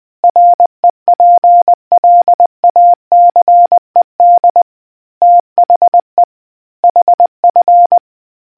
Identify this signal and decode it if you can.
Morse code, heard as a keyed tone, about 20 wpm, 700 Hz. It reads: REPLACED THE HF